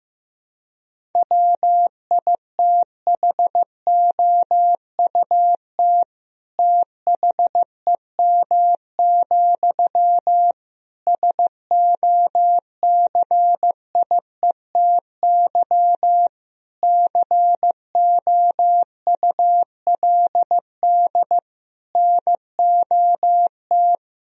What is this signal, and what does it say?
Morse code, heard as a keyed tone, about 15 wpm, 700 Hz. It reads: WITHOUT THEM, SOCIETY COULD NOT
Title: Sentences from Top 1000 Words - Review Part 4 - 15wpm